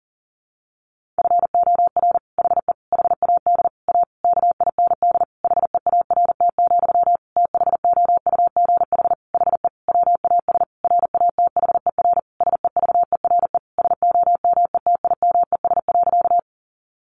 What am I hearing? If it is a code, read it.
Morse code, heard as a keyed tone, about 37 wpm, 700 Hz. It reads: FOR HE HAD A KIND HEART, THOUGH HE WAS RATHER SEVERE SOMETIMES.
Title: Sentences from Top 1400 Words 40wpm